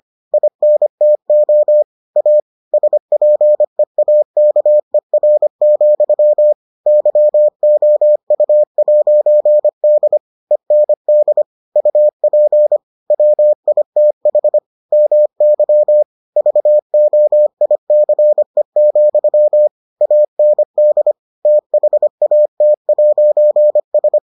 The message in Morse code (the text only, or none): INTO A SPEAKER, YOU'D END UP WITH MY VOICE, AND THAT'S